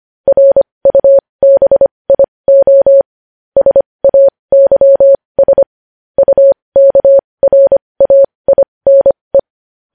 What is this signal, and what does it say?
Morse code, heard as a keyed tone, about 25 wpm, spaced as for 20 wpm, 550 Hz. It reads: RUBIO SAYS UKRAINE
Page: News Headlines in Morse Code at 20 WPM